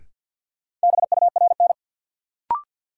Morse code, heard as a keyed tone, about 53 wpm, 700 Hz. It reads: BURN
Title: Sentences from Top 1300 Words - New Word Drill Alt - 50wpm